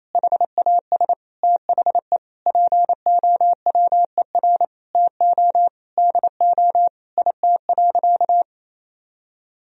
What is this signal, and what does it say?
Morse code, heard as a keyed tone, about 28 wpm, 690 Hz. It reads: HAS THE POWER TO DO IT.
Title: Sentences from Top 1000 Words - Review Part 3 - 28wpm